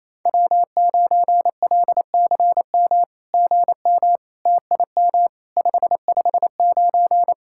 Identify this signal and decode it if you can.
Morse code, heard as a keyed tone, about 27 wpm, 700 Hz. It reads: W9LCM GM TIM 559
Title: SOTA QSO Practice 28wpm